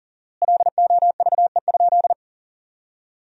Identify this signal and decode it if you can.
Morse code, heard as a keyed tone, about 40 wpm, 700 Hz. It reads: LOVE?
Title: 4x Word-Spacing - Sentences from Top 400 Words - 40wpm